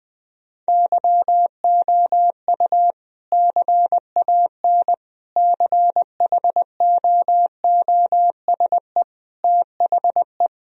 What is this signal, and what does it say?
Morse code, heard as a keyed tone, about 20 wpm, 700 Hz. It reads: YOU CAN CHOOSE THE
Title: Sentences from Top 1000 Words - Review Part 4 - 20wpm